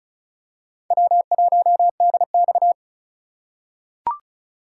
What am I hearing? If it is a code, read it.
Morse code, heard as a keyed tone, about 36 wpm, 700 Hz. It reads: W1DX
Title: K1USN SST QSO Practice - 35wpm